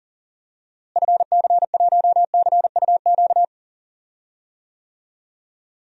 Morse code, heard as a keyed tone, about 40 wpm, 700 Hz. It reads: FC1CUQ
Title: DXCC-100 Call Signs - 6 Characters - Repeated 3x - 40wpm